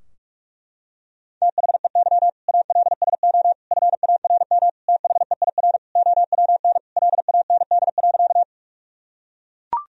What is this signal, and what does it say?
Morse code, heard as a keyed tone, about 41 wpm, 700 Hz. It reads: THEY ALSO FARM THEIR OWN LAND.
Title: Sentences from Top 1000 Words - Review Part 2 - 45wpm